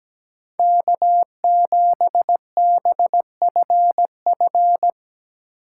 Morse code, heard as a keyed tone, about 17 wpm, 700 Hz. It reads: K7BFF
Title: USA Call Signs - 5 Characters - 2x Character Spacing - 17wpm